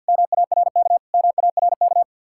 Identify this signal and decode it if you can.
Morse code, heard as a keyed tone, about 61 wpm, 700 Hz. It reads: MARK MARK
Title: Speed-Racing Common QSO Phrases - 50wpm